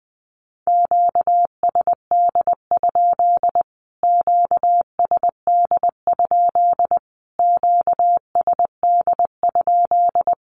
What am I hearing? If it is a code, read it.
Morse code, heard as a keyed tone, about 20 wpm, 700 Hz. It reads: QSD? QSD? QSD?
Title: Q-Codes Repeated 3x 20wpm